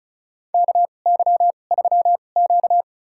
Morse code, heard as a keyed tone, about 34 wpm, spaced as for 25 wpm, 700 Hz. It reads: KY3Q
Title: USA Call Signs - 4 Characters - 2x Character Spacing - 35wpm